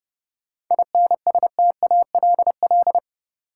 Morse code, heard as a keyed tone, about 30 wpm, 700 Hz. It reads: INSTALL
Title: Sentences from Top 1600 Words - New Word Drill Alt - 30wpm